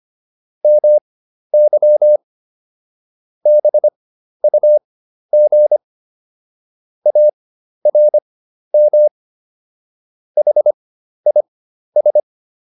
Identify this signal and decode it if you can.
Morse code, heard as a keyed tone, about 25 wpm, spaced as for 12 wpm, 600 Hz. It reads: MY BUG ARM HIS